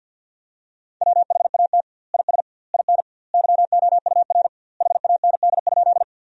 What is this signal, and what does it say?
Morse code, heard as a keyed tone, about 49 wpm, 700 Hz. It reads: WHAT IS IN YOUR HAND?
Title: 2x Word-Spacing - Sentences from Top 200 Words - 50wpm